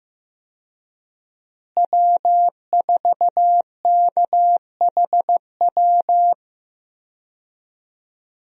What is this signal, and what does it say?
Morse code, heard as a keyed tone, about 15 wpm, 700 Hz. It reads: W4KHW
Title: USA Call Signs - 5 Characters - Repeated 2x - 15wpm